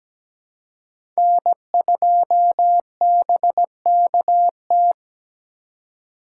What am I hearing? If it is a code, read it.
Morse code, heard as a keyed tone, about 17 wpm, 700 Hz. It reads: N2BKT
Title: USA Call Signs - 5 Characters - 3x Character Spacing - 17wpm